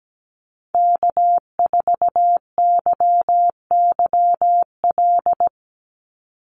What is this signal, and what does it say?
Morse code, heard as a keyed tone, about 17 wpm, 700 Hz. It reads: K4YYL
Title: Call Signs Repeated 2x - 17wpm